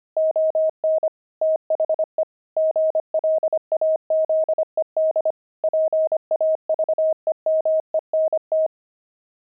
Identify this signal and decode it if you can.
Morse code, heard as a keyed tone, about 25 wpm, 630 Hz. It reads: ON THE GLAZED PAVEMENT
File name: pangram-fs-25-cs-25-pavement